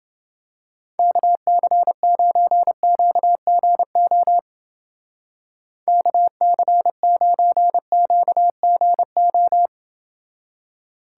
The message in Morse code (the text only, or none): KC9QGO KC9QGO